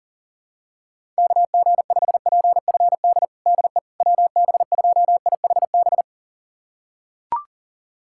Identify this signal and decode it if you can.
Morse code, heard as a keyed tone, about 37 wpm, 700 Hz. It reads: KG5PFD DE WB2IHB
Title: USA Callsign DE Callsign 40wpm